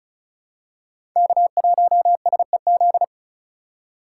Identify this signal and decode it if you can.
Morse code, heard as a keyed tone, about 36 wpm, 700 Hz. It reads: K1SEZ
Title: USA Call Signs - 5 Characters - 4x Character Spacing - 35wpm